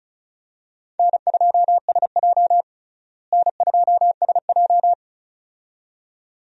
Morse code, heard as a keyed tone, about 36 wpm, 700 Hz. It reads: N2SJ N2SJ
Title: USA Call Signs - 4 Characters - Repeated 2x - 35wpm